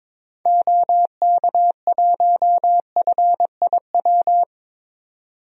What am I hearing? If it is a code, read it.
Morse code, heard as a keyed tone, about 22 wpm, 700 Hz. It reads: OK1FIW
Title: DXCC-100 Call Signs - 6 Characters - 22wpm